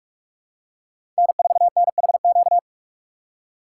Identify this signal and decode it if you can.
Morse code, heard as a keyed tone, about 46 wpm, 690 Hz. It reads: N4NHQ
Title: USA Call Signs - 5 Characters - 6x Character Spacing - 45wpm